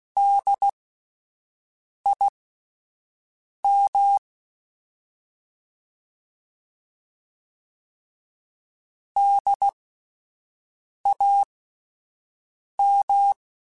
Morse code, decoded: DIM DAM

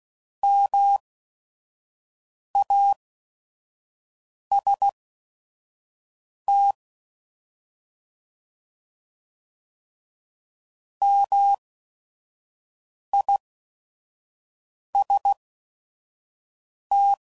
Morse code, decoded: MAST MIST

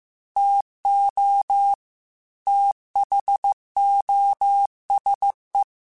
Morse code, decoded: TO THOSE

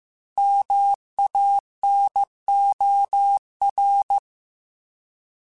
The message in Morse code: MANOR

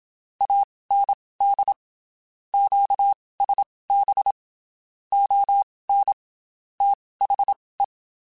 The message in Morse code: AND QSB ON THE

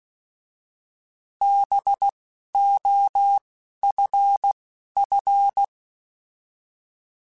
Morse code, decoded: B O F F

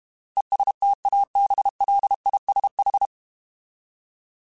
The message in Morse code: ESTABLISH

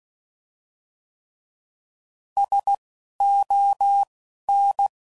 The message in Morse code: SON

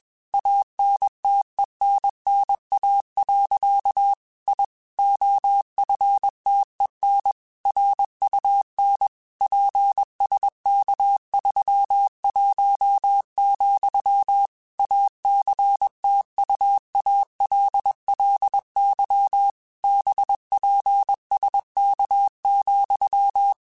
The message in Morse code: ANTENNA. I OFTEN RUN PSK31, ACTUALLY BPSK,

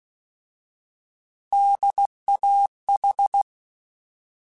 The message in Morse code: DAH

